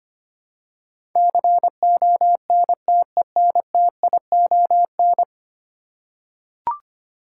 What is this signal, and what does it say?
Morse code, heard as a keyed tone, about 25 wpm, 700 Hz. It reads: CONTENTION
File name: Ten Letter Words - 25wpm